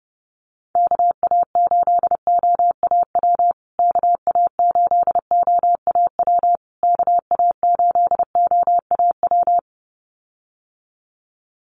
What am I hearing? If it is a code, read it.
Morse code, heard as a keyed tone, about 30 wpm, 700 Hz. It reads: KA8OAW KA8OAW KA8OAW
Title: USA Call Signs Repeated 3x 30wpm